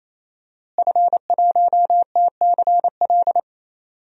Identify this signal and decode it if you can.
Morse code, heard as a keyed tone, about 28 wpm, 700 Hz. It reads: F1TCL
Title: DXCC-100 Call Signs - 5 Characters - Repeated 3x - 28wpm